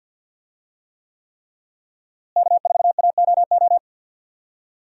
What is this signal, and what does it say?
Morse code, heard as a keyed tone, about 53 wpm, 700 Hz. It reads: K4AOO
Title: USA Call Signs - 5 Characters - Repeated 3x - 50wpm